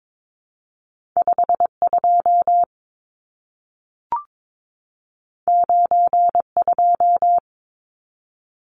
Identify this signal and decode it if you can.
Morse code, heard as a keyed tone, about 22 wpm, 700 Hz. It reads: 52 92
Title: Numbers - 2-Digit 22wpm